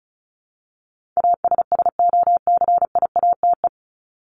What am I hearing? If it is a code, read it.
Morse code, heard as a keyed tone, about 34 wpm, 700 Hz. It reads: ASSOCIATE
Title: ED Suffix - Top 100 Words with Base Words 35wpm